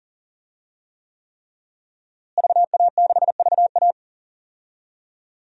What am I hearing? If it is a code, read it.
Morse code, heard as a keyed tone, about 40 wpm, 700 Hz. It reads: VA6VA
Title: DXCC-100 Call Signs - 5 Characters - Repeated 3x - 40wpm